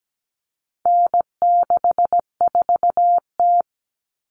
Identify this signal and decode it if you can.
Morse code, heard as a keyed tone, about 17 wpm, 700 Hz. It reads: N64T
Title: Common Cut Numbers - 4-Digit 17wpm